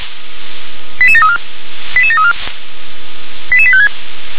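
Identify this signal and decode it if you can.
Five-tone selective call (ZVEI), heard as quick runs of five tones, tones of about 70 ms. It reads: 88734 88734 88856
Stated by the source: Five-tone ZVEI selective call received on an SDR.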